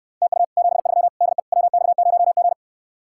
Morse code, heard as a keyed tone, about 69 wpm, 700 Hz. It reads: TU 73 DE WX0G